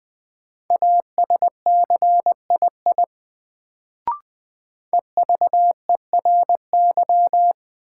Morse code, heard as a keyed tone, about 20 wpm, 700 Hz. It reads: ASCII EVERY
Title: Sets of 5 Words - New Word Drill - 20wpm